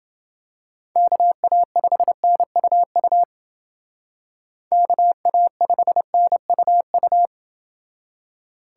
Morse code, heard as a keyed tone, about 29 wpm, 700 Hz. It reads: KA5NUU KA5NUU